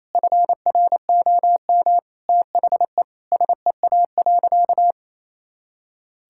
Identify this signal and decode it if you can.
Morse code, heard as a keyed tone, about 28 wpm, 700 Hz. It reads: FROM THE SEA.
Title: Sentences from Top 1000 Words - Review Part 6 - 28wpm